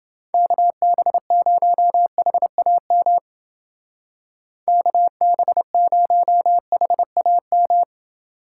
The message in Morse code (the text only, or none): KB0HAM KB0HAM